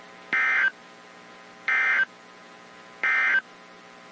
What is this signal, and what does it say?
Emergency Alert System (SAME) tones, heard as harsh buzzing bursts. It is heard from 0.2 to 3.5 s.
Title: WZ2523 - NOAA Weather Radio